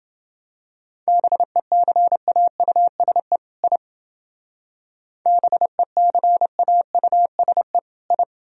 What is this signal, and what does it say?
Morse code, heard as a keyed tone, about 29 wpm, 700 Hz. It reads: BECAUSE I BECAUSE I